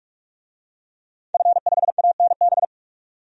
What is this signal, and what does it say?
Morse code, heard as a keyed tone, about 46 wpm, 700 Hz. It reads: F5ANB
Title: DXCC-100 Call Signs - 5 Characters - 45wpm